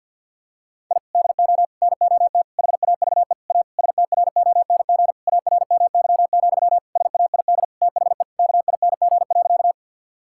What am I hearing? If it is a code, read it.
Morse code, heard as a keyed tone, about 46 wpm, 700 Hz. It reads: I DO NOT HAVE A STRONG ARMY, SAID THE KING.
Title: Sentences from Top 1000 Words - Review Part 5 - 50wpm